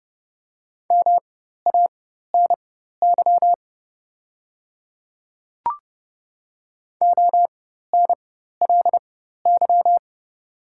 Morse code, heard as a keyed tone, about 30 wpm, spaced as for 14 wpm, 700 Hz. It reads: MANY ONLY
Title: 4x Character Spacing - Top 500 Words - 30wpm